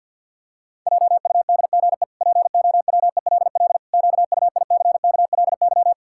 Morse code, heard as a keyed tone, about 50 wpm, 700 Hz. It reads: JUDGE POWELL QUICKLY
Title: Pangram Challenge v2 - 50wpm